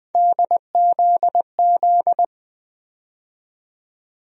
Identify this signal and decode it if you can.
Morse code, heard as a keyed tone, about 21 wpm, 700 Hz. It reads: DZZ
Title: USA Call Sign Postfixes - 3 Letters 20wpm